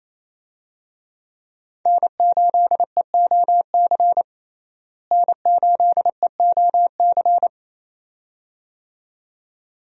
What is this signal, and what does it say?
Morse code, heard as a keyed tone, about 28 wpm, 700 Hz. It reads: N8EOC N8EOC
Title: USA Call Signs - 5 Characters - Repeated 2x - 28wpm